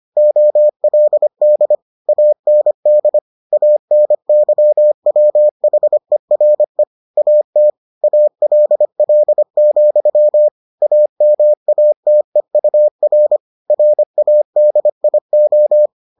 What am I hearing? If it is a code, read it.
Morse code, heard as a keyed tone, about 25 wpm, 600 Hz. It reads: OLD AND ANYWHERE AT ALL, AMATEUR RADIO